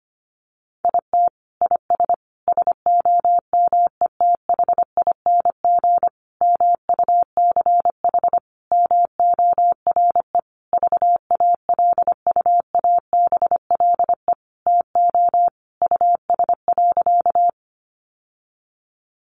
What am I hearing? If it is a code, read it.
Morse code, heard as a keyed tone, about 25 wpm, 700 Hz. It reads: IT IS SOMETHING MUCH MORE VALUABLE TO US.
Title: Sentences from Top 1100 Words 25wpm